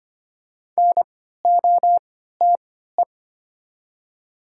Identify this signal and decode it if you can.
Morse code, heard as a keyed tone, about 25 wpm, spaced as for 14 wpm, 700 Hz. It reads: NOTE
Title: 3x Character Spacing - Top 500 Words - 25wpm